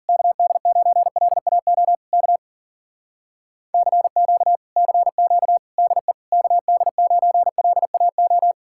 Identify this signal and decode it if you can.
Morse code, heard as a keyed tone, about 52 wpm, 700 Hz. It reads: KD9LAO K CQ CQ DE KD9LAO